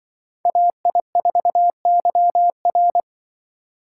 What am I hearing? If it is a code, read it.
Morse code, heard as a keyed tone, about 25 wpm, 700 Hz. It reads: AI4YR
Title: Speed Racing - USA Call Signs 20wpm